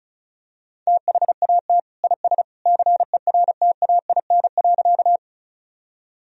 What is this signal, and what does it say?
Morse code, heard as a keyed tone, about 34 wpm, 700 Hz. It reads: THAT IS CERTAIN.
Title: Sentences from Top 1000 Words - Review Part 2 - 35wpm